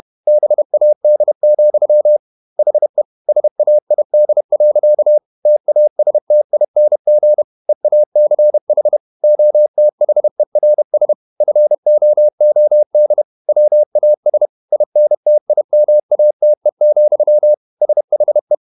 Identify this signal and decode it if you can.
Morse code, heard as a keyed tone, about 31 wpm, 600 Hz. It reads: BAD, HE SAID. TASTING EACH OTHERS FOOD WAS INTIMATE, SHE
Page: Never 31wpm | CW med Gnister